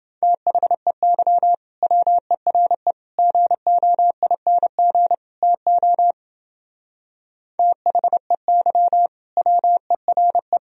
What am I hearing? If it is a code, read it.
Morse code, heard as a keyed tone, about 30 wpm, 700 Hz. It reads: THEY WERE GOING TO THEY WERE